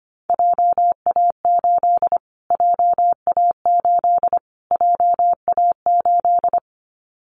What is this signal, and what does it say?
Morse code, heard as a keyed tone, about 25 wpm, 700 Hz. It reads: JA8 JA8 JA8
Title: Callsign Prefixes for Top 25 DXCC - Repeated 3x - 25wpm